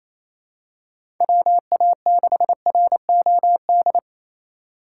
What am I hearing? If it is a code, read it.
Morse code, heard as a keyed tone, about 28 wpm, 700 Hz. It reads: WA6ROD
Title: USA Call Signs - 6 Characters - 2x Character Spacing - 28wpm